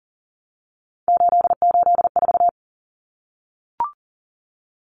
Morse code, heard as a keyed tone, about 40 wpm, 700 Hz. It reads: 884
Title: Common Cut Numbers - 3-Digit 40wpm